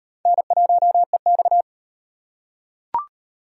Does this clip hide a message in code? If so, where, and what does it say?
Yes: Morse code from 0.3 to 1.6 s, heard as a keyed tone, about 38 wpm, 700 Hz. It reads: N1EX